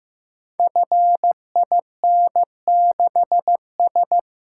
Morse code, heard as a keyed tone, about 15 wpm, 700 Hz. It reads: FIN6S
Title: Code Groups - 5 Letter:Numbers - 15wpm